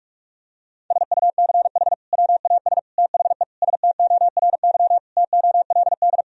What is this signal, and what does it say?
Morse code, heard as a keyed tone, about 46 wpm, 700 Hz. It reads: SUCH WAS THE STORY TOLD
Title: Sentences from Top 1000 Words - Review Part 3 - 45wpm